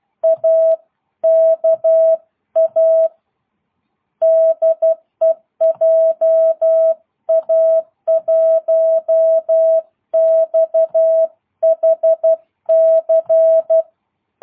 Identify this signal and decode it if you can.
Morse code, heard as a keyed tone, about 12 wpm, 650 Hz. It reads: AKA DE JA1XHC
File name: jh8cbh / 2022cw